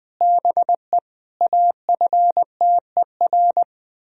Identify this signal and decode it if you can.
Morse code, heard as a keyed tone, about 20 wpm, 700 Hz. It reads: BE AFTER
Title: Sentences from Top 1000 Words - Review Part 1 - 20wpm